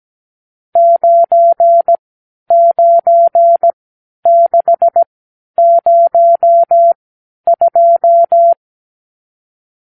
Morse code, heard as a keyed tone, about 17 wpm, spaced as for 10 wpm, 680 Hz. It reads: 99602
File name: arrl / Review (1-0)